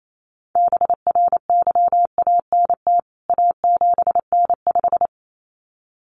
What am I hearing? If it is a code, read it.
Morse code, heard as a keyed tone, about 28 wpm, 700 Hz. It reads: BRYANT A7N5
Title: CWT Exchange 28wpm